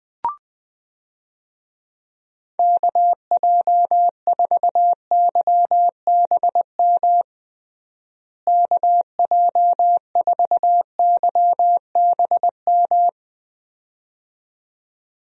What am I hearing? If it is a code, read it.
Morse code, heard as a keyed tone, about 21 wpm, 700 Hz. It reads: KJ4YBM KJ4YBM
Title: USA Call Signs - 6 Characters - Repeated 2x - 20wpm